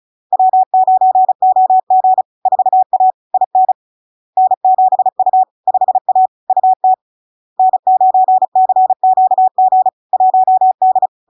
Morse code, heard as a keyed tone, about 36 wpm, 750 Hz. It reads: W9OG 4A IN N7U 5A UT N9CQG 1D
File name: LARK-FD-2022-Log-in-CW-35wpm-750-Hz